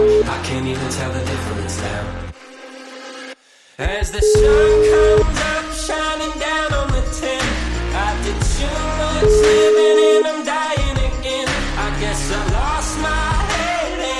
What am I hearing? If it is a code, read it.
Ringback tone (Europe, 425 Hz), with noise.